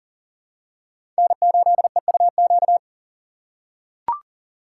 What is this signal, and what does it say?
Morse code, heard as a keyed tone, about 40 wpm, 700 Hz. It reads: N8EUQ
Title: USA Call Signs - 5 Characters - 3x Character Spacing - 40wpm